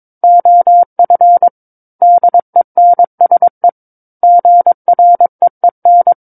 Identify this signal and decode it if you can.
Morse code, heard as a keyed tone, about 22 wpm, 700 Hz. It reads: OF DENSE GREEN
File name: Book One / 13-Chapter 13 - 22 WPM